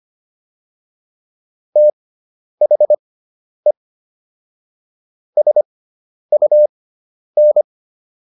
Morse code, heard as a keyed tone, about 25 wpm, spaced as for 10 wpm, 600 Hz. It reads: THE SUN